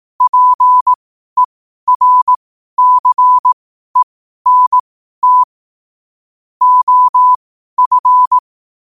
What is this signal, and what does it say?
Morse code, heard as a keyed tone, about 18 wpm, spaced as for 12 wpm, 1000 Hz. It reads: PERCENT OF
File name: qotdfiles / 12wpm